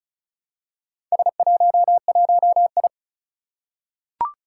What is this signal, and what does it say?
Morse code, heard as a keyed tone, about 34 wpm, 700 Hz. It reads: S11I
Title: Four Challenging Letter-Number Combinations 35wpm